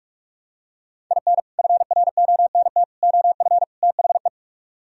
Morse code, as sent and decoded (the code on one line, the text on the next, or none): .. -. / ..-. .-. --- -. - / --- ..-. / - .... .
IN FRONT OF THE